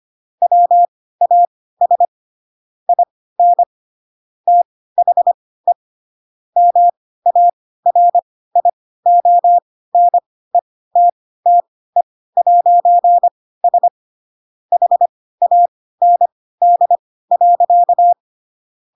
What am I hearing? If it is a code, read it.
Morse code, heard as a keyed tone, about 25 wpm, spaced as for 16 wpm, 700 Hz. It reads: WAS IN THE MARIONETTE'S HAND.
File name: Pinocchio / Pinocchio ch3 10 25 ik1hge